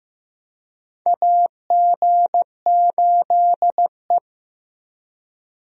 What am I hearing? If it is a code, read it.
Morse code, heard as a keyed tone, about 15 wpm, 700 Hz. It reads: AG8E
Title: USA Call Signs - 4 Characters - Repeated 2x - 15wpm